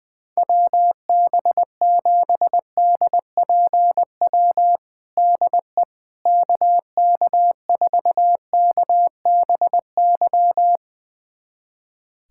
Morse code, heard as a keyed tone, about 20 wpm, 700 Hz. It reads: WB7DPW DE KK4KBY
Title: USA Callsign DE Callsign - 6 Characters - 20wpm